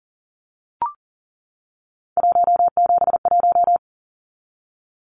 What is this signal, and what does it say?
Morse code, heard as a keyed tone, about 40 wpm, 700 Hz. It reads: 171
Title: Numbers - 3-Digit 40wpm